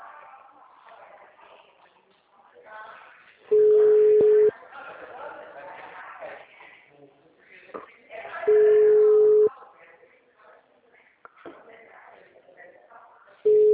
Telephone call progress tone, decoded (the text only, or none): ringback tone